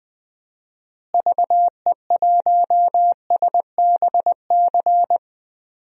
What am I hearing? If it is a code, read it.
Morse code, heard as a keyed tone, about 20 wpm, 700 Hz. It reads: VE1SBC
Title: DXCC-100 Call Signs - 6 Characters - Repeated 2x - 20wpm